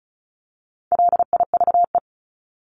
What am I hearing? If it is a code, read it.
Morse code, heard as a keyed tone, about 34 wpm, 700 Hz. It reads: LIVE
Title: ED Suffix - Top 100 Words with Base Words 35wpm